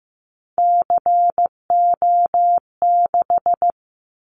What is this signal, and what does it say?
Morse code, heard as a keyed tone, about 15 wpm, 700 Hz. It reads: CO6
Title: Callsign Prefixes for Top 100 DXCC - 15wpm